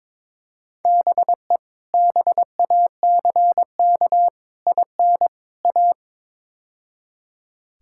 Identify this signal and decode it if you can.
Morse code, heard as a keyed tone, about 22 wpm, 700 Hz. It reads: BE BACK IN A
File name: practice / 22wpm / Build A Phrase - 5-Word Phrases - 22wpm